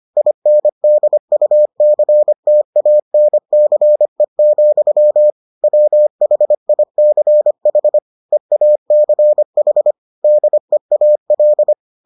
Morse code, heard as a keyed tone, about 25 wpm, 600 Hz. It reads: INDUCTANCE, WHICH EACH DEAL